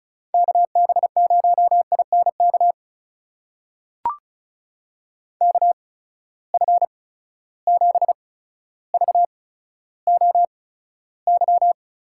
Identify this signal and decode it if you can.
Morse code, heard as a keyed tone, about 36 wpm, spaced as for 15 wpm, 700 Hz. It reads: KB0INK KF7VOY
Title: USA Call Signs - 6 Characters - 8x Character Spacing - 35wpm